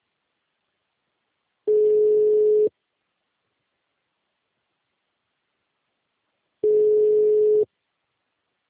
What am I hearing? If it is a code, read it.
Ringback tone (Europe, 425 Hz).